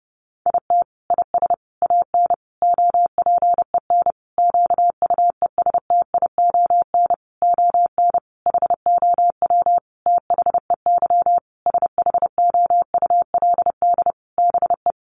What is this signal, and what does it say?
Morse code, heard as a keyed tone, about 30 wpm, 700 Hz. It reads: IT IS AN OPEN QUESTION ON HOW THEY SHOULD BE
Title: Sentences from Top 1300 Words 30wpm